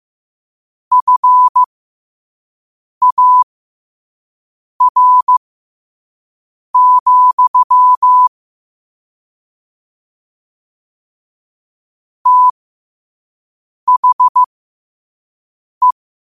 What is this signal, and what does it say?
Morse code, heard as a keyed tone, about 15 wpm, spaced as for 5 wpm, 1000 Hz. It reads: FAR, THE
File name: qotdfiles / 5wpm